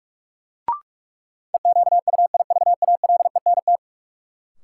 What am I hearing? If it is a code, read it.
Morse code, heard as a keyed tone, about 41 wpm, 700 Hz. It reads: EQUIVALENT
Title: Sentences from Top 1600 Words - New Word Drill - 45wpm